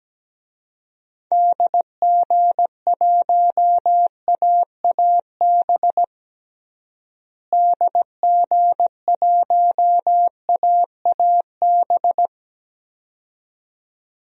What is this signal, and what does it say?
Morse code, heard as a keyed tone, about 17 wpm, 700 Hz. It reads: DG1AAB DG1AAB
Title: DXCC-100 Call Signs - 6 Characters - Repeated 2x - 17wpm